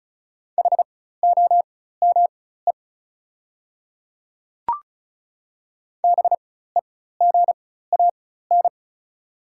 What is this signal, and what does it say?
Morse code, heard as a keyed tone, about 36 wpm, spaced as for 16 wpm, 690 Hz. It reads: HOME BEGAN
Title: 4x Character Spacing - Top 300 Words - 35wpm